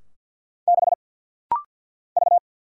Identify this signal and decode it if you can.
Morse code, heard as a keyed tone, about 48 wpm, 700 Hz. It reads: 6 V
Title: Single Letter-Number - Rapid-Fire 50wpm